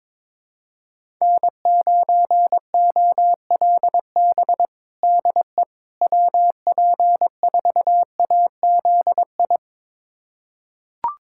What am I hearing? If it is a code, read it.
Morse code, heard as a keyed tone, about 22 wpm, 700 Hz. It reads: N9OLB DE WP4AZI
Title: USA Callsign DE Callsign 22wpm